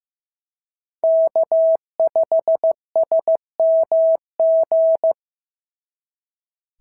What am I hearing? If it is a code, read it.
Morse code, heard as a keyed tone, about 14 wpm, 650 Hz. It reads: K5SMG
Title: SOTA QSO Practice 15wpm